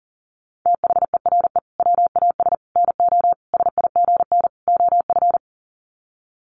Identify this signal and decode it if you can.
Morse code, heard as a keyed tone, about 40 wpm, 700 Hz. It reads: THERE WAS NO SIGN OF
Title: Sets of 5 Words Encore2 40wpm